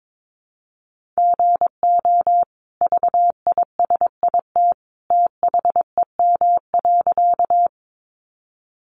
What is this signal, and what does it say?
Morse code, heard as a keyed tone, about 22 wpm, 700 Hz. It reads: GO VISIT THEM.
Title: Sentences from Top 900 Words 22wpm